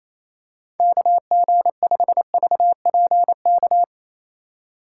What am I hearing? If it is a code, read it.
Morse code, heard as a keyed tone, about 27 wpm, 700 Hz. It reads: KG5VPK
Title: USA Call Signs - 6 Characters - 4x Character Spacing - 28wpm